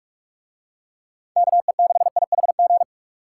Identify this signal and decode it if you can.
Morse code, heard as a keyed tone, about 44 wpm, 700 Hz. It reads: KE6IHG